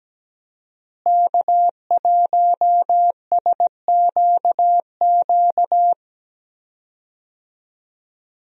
Morse code, heard as a keyed tone, about 17 wpm, 700 Hz. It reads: K1SQQ
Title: USA Call Signs - 5 Characters - Repeated 3x - 17wpm